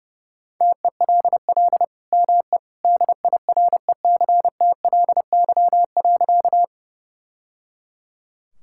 Morse code, heard as a keyed tone, about 30 wpm, 700 Hz. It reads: TELL ME DIRECTLY.
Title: Sentences from Top 1000 Words - Review Part 5 - 30wpm